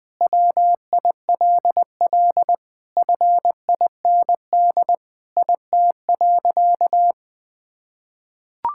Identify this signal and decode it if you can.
Morse code, heard as a keyed tone, about 20 wpm, 700 Hz. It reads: WILL FIND IT.
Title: Sentences with 3-Word Phrases - 20wpm